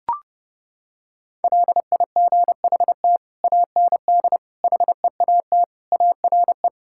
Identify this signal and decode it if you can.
Morse code, heard as a keyed tone, about 32 wpm, 700 Hz. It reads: LIGHT AND HEAT ARE
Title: Sentences from Top 1000 Words - Review Part 5 - 30wpm